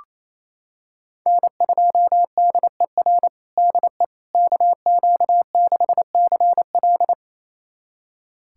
Morse code, heard as a keyed tone, about 27 wpm, 700 Hz. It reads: N2DER DE KQ6CL
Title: USA Callsign DE Callsign - 5 Characters - 28wpm